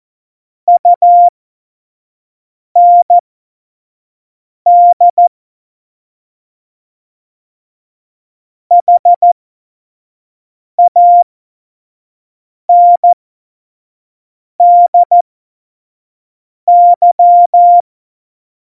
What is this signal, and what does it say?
Morse code, heard as a keyed tone, about 14 wpm, spaced as for 5 wpm, 700 Hz. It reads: UND HANDY
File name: Morsekurs / Witze / Witz 28 April 2020, 25 F70